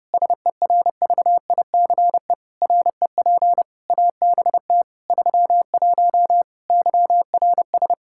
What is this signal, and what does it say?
Morse code, heard as a keyed tone, about 30 wpm, 700 Hz. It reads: SERVICE REP ABT 31 YRS
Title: Practice QSO - Exchange 3 - More Info - 30wpm